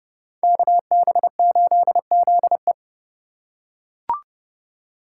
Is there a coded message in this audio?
Yes: Morse code from 0.4 to 2.7 s, heard as a keyed tone, about 29 wpm, 700 Hz. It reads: KB8ZE